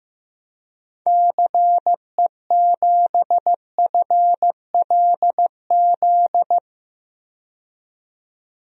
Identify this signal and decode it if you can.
Morse code, heard as a keyed tone, about 15 wpm, 700 Hz. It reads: CE7FLZ
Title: DXCC-100 Call Signs - 6 Characters - 15wpm